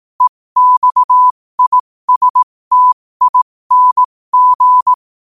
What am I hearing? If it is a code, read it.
Morse code, heard as a keyed tone, about 18 wpm, spaced as for 15 wpm, 1000 Hz. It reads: EXISTING
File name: qotdfiles / 15wpm